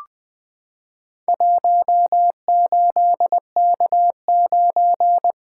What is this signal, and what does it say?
Morse code, heard as a keyed tone, about 21 wpm, 700 Hz. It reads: 18K9
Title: Four Letter-Number Combinations - 20wpm